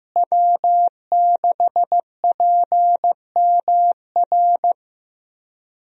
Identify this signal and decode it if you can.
Morse code, heard as a keyed tone, about 16 wpm, 700 Hz. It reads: W6PMR
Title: USA Call Signs - 5 Characters - Repeated 3x - 15wpm